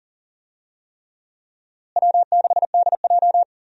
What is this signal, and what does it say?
Morse code, heard as a keyed tone, about 40 wpm, 700 Hz. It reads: W6DJ
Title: USA Call Signs - 4 Characters - Repeated 3x - 40wpm